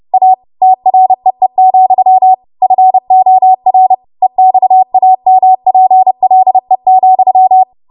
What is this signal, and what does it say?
Morse code, heard as a keyed tone, about 30 wpm, 750 Hz. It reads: A TREE, FOR EXAMPLE,
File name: Morse / 30WPM